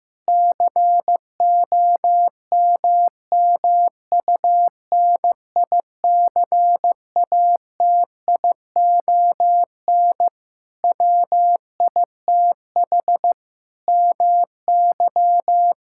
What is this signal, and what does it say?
Morse code, heard as a keyed tone, about 15 wpm, 700 Hz. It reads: COMMUNICATION WITH MY
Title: Sentences from Top 1000 Words - Review Part 6 - 15wpm